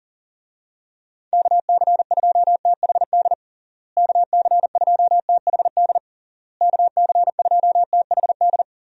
Morse code, heard as a keyed tone, about 40 wpm, 700 Hz. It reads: KC2THD KC2THD KC2THD
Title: USA Call Signs - 6 Characters - Repeated 3x - 40wpm